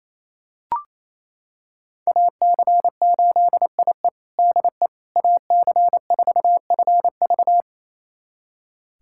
Morse code, heard as a keyed tone, about 28 wpm, 700 Hz. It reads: AC8IE DE AC4FV
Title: USA Callsign DE Callsign - 5 Characters - 28wpm